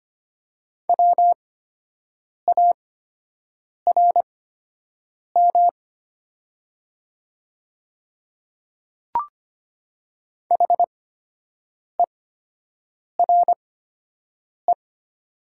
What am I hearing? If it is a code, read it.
Morse code, heard as a keyed tone, about 25 wpm, spaced as for 7 wpm, 700 Hz. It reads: WARM HERE
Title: 8x Character Spacing - Top 500 Words - 25wpm